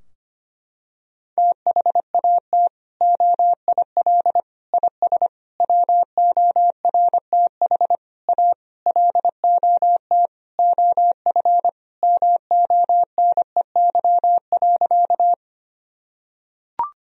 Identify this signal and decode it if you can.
Morse code, heard as a keyed tone, about 25 wpm, 700 Hz. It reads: THAT OIL IS WORTH A LOT OF MONEY.
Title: Sentences from Top 1000 Words - Review Part 4 - 25wpm